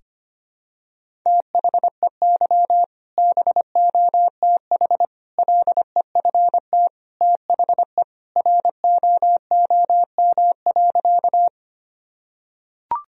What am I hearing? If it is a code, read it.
Morse code, heard as a keyed tone, about 25 wpm, 700 Hz. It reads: THEY BOTH LEFT THE ROOM.
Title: Sentences from Top 1000 Words - Review Part 1 - 25wpm